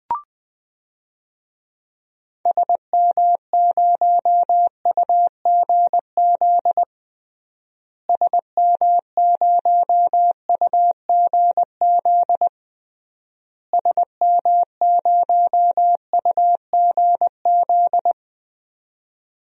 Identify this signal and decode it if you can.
Morse code, heard as a keyed tone, about 21 wpm, 700 Hz. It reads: SM0UGZ SM0UGZ SM0UGZ
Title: DXCC-100 Call Signs - 6 Characters - Repeated 3x - 20wpm